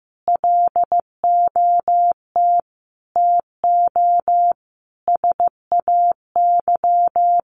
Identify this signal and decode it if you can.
Morse code, heard as a keyed tone, about 15 wpm, 700 Hz. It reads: LOT TO SAY
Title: Sentences from Top 1000 Words 15wpm